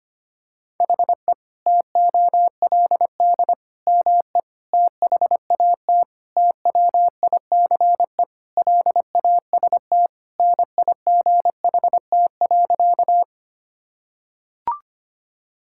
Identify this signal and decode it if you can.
Morse code, heard as a keyed tone, about 25 wpm, 700 Hz. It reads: HE TOLD ME THAT TWICE LAST NIGHT.
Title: Sentences from Top 1600 Words 25wpm